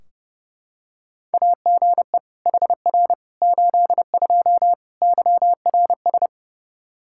Morse code, heard as a keyed tone, about 32 wpm, 700 Hz. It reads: AGE HR 82 YRS
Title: K1USN SST QSO Practice 30wpm